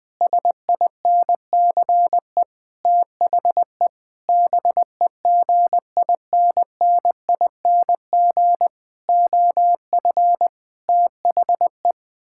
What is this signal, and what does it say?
Morse code, heard as a keyed tone, about 20 wpm, 700 Hz. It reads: SINCE THE BEGINNING OF THE
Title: Sentences from Top 1000 Words - Review Part 6 - 20wpm